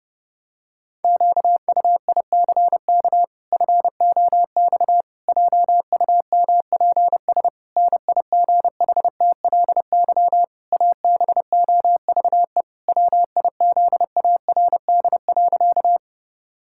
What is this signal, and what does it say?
Morse code, heard as a keyed tone, about 32 wpm, 700 Hz. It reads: QUICK FOX JUMPS NIGHTLY ABOVE WIZARD.
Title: Pangram Challenge v2 - 30wpm